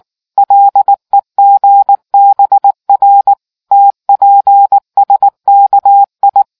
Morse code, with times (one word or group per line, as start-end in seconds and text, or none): 0.4-3.3 LEGBR
3.7-6.4 TPSKI